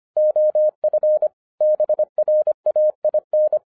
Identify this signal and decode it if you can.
Morse code, heard as a keyed tone, about 25 wpm, 610 Hz. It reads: OF BRAIN
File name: 2011-07-mp3 / n59 BACK FROM A BIG ACCIDENT 25wpm